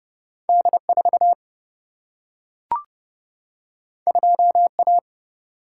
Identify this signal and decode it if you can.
Morse code, heard as a keyed tone, about 30 wpm, 700 Hz. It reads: D 4 2 A
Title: Single-Letter Single-Number 30wpm